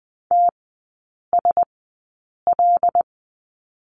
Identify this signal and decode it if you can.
Morse code, heard as a keyed tone, about 20 wpm, spaced as for 8 wpm, 700 Hz. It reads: TSL
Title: 127 Lesson 21e - Number 6 - Three Characters 20wpm